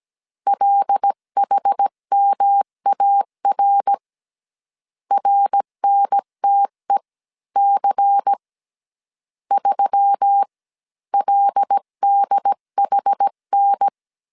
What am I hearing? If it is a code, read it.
Morse code, heard as a keyed tone, about 17 wpm, spaced as for 14 wpm, 800 Hz. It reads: LHMAR RNTEC 3LDHN
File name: CW / DET / 1 6 1500C S C 16wpm 2733